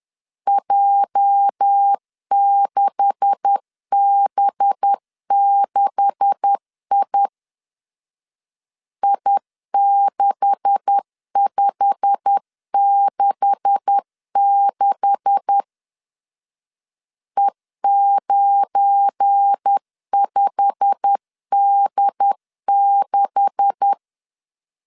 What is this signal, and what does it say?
Morse code, heard as a keyed tone, about 10 wpm, 800 Hz. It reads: J6B6I I6566 E95D6
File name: LEZIONI / 9 286 10wpm 2735